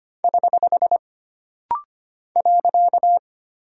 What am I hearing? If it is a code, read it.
Morse code, heard as a keyed tone, about 25 wpm, 700 Hz. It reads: <HH> .
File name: Single Punctuation-Prosign - Rapid-Fire - 25wpm